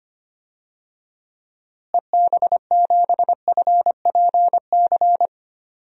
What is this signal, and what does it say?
Morse code, heard as a keyed tone, about 25 wpm, 700 Hz. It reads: EB7FPC
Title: DXCC-100 Call Signs - 6 Characters - Repeated 3x - 25wpm